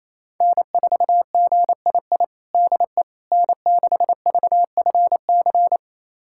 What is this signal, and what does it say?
Morse code, heard as a keyed tone, about 27 wpm, 700 Hz. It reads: N4GII DE N6VFC
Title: USA Callsign DE Callsign - 5 Characters - 28wpm